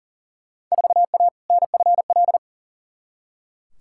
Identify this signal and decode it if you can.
Morse code, heard as a keyed tone, about 40 wpm, 700 Hz. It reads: 4A NFL
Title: Field Day Exchanges - 40wpm